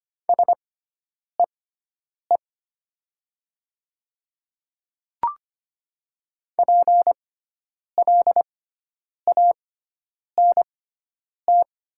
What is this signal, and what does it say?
Morse code, heard as a keyed tone, about 25 wpm, spaced as for 9 wpm, 700 Hz. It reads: SEE PLANT
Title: 6x Character Spacing - Top 300 Words - 25wpm